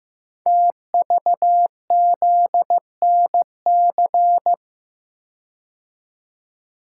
Morse code, heard as a keyed tone, about 15 wpm, 700 Hz. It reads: TVZNC
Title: Code Groups - 5 Letter:Numbers - 15wpm